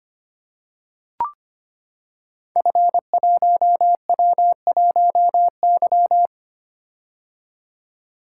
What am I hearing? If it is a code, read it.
Morse code, heard as a keyed tone, about 25 wpm, 700 Hz. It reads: F1W1Y
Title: Code Groups - 5 Letter:Numbers - 25wpm